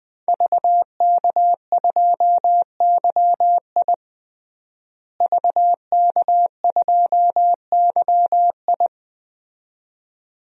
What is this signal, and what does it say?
Morse code, heard as a keyed tone, about 21 wpm, 700 Hz. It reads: VK2YI VK2YI
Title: DXCC-100 Call Signs - 5 Characters - Repeated 2x - 20wpm